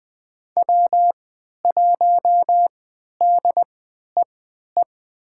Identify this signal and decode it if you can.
Morse code, heard as a keyed tone, about 20 wpm, spaced as for 11 wpm, 700 Hz. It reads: W1DEE
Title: USA Call Signs - 5 Characters - 3x Character Spacing - 20wpm